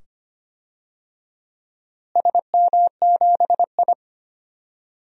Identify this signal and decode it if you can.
Morse code, heard as a keyed tone, about 25 wpm, 700 Hz. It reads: SM7I
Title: DXCC-100 Call Signs - 4 Characters - Repeated 2x - 25wpm